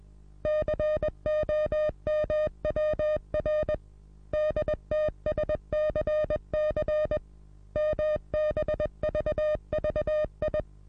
Morse code, heard as a keyed tone, about 21 wpm, 610 Hz. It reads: COMWR DTSCC MBVVI
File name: repos / Morsekurs / DJ4UF Morsen26